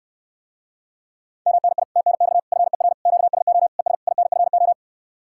Morse code, heard as a keyed tone, about 69 wpm, 700 Hz. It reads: MNI TNX FER QSO ES INFO